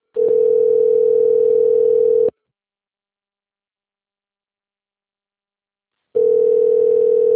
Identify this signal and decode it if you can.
Ringback tone (North America), 440 + 480 Hz.
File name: rg-1600-unknown- - - .14112